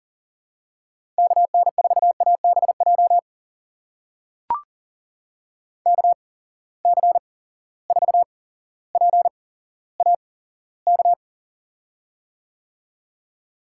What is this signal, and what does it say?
Morse code, heard as a keyed tone, about 40 wpm, spaced as for 17 wpm, 700 Hz. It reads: KN4ABJ KC4PAK
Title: USA Call Signs - 6 Characters - 8x Character Spacing - 40wpm